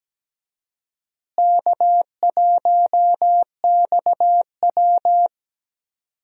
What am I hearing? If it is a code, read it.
Morse code, heard as a keyed tone, about 17 wpm, 700 Hz. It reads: K1XW
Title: USA Call Signs - 4 Characters - Repeated 2x - 17wpm